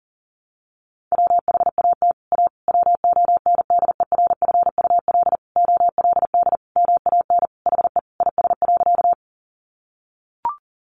Morse code, heard as a keyed tone, about 40 wpm, 700 Hz. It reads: WHAT A WONDERFUL OLD MAN HE IS.
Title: Sentences from Top 1100 Words 40wpm